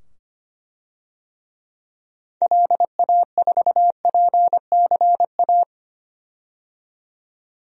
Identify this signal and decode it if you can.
Morse code, heard as a keyed tone, about 25 wpm, 700 Hz. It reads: LA4PCA
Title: DXCC-100 Call Signs - 6 Characters - 25wpm